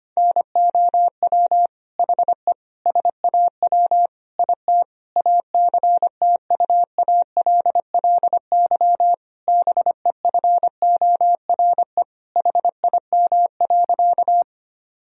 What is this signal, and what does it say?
Morse code, heard as a keyed tone, about 25 wpm, 700 Hz. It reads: NOW HE SAW IT ACTUALLY BEFORE HIM.
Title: Sentences from Top 1000 Words - Review Part 4 - 25wpm